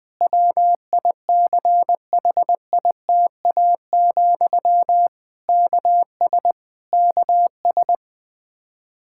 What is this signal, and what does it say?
Morse code, heard as a keyed tone, about 20 wpm, 700 Hz. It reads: WICHITA, KS KS
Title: Practice QSO - Exchange 1 - Beginning - 20wpm